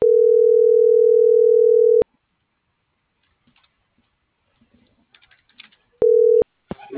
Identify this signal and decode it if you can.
Ringback tone (North America), 440 + 480 Hz.